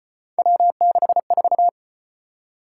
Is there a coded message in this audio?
Yes: Morse code from 0.4 to 1.7 s, heard as a keyed tone, about 34 wpm, 700 Hz. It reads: W64